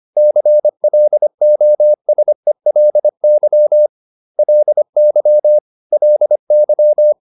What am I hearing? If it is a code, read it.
Morse code, heard as a keyed tone, about 25 wpm, 600 Hz. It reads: CLOSELY LY LY